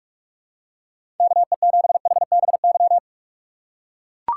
Morse code, heard as a keyed tone, about 41 wpm, 700 Hz. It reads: KE7HBY
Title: USA Call Signs - 6 Characters - 2x Character Spacing - 45wpm